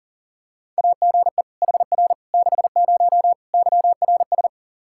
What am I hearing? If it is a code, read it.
Morse code, heard as a keyed tone, about 40 wpm, 700 Hz. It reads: AGE HR 60 YRS
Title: Practice QSO - Exchange 3 - More Info - 40wpm